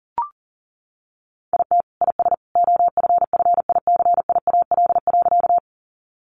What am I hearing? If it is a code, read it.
Morse code, heard as a keyed tone, about 40 wpm, 700 Hz. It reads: IT IS OFFICIAL.
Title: Sentences from Top 1000 Words 40wpm